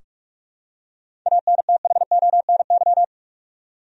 Morse code, heard as a keyed tone, about 46 wpm, 700 Hz. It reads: ANTHONY
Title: USA Common Names - 45wpm